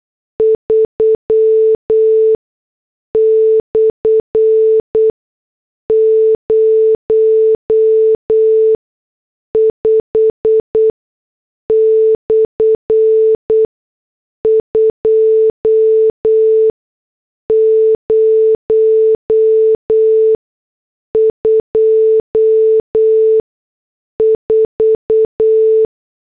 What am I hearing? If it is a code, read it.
Morse code, heard as a keyed tone, about 8 wpm, spaced as for 6 wpm, 440 Hz. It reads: 3/05/2024